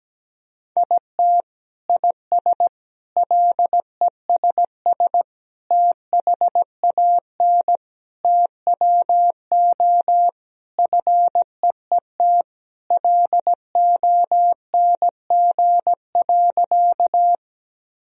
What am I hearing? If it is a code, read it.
Morse code, heard as a keyed tone, about 17 wpm, 700 Hz. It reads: IT IS LESS THAN TWO FEET LONG.
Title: Sentences from Top 1000 Words - Review Part 2 - 17wpm